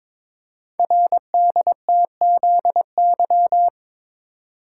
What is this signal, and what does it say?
Morse code, heard as a keyed tone, about 22 wpm, 700 Hz. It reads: RDTZY
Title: Code Groups - 5 Letters - 22wpm